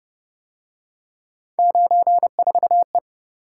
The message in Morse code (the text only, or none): 94E